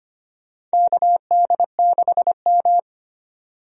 Morse code, heard as a keyed tone, about 25 wpm, 700 Hz. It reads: KD6M
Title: USA Call Signs - 4 Characters - 25wpm